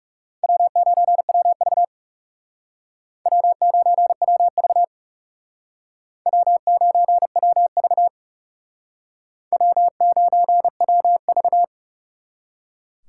Morse code, heard as a keyed tone, about 40 wpm, 700 Hz. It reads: W9WV W9WV W9WV W9WV